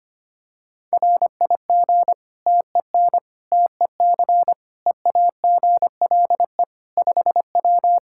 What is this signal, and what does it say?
Morse code, heard as a keyed tone, about 25 wpm, 700 Hz. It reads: RIG TEN TEC EAGLE 5W
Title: Practice QSO - Exchange 2 - Info - 25wpm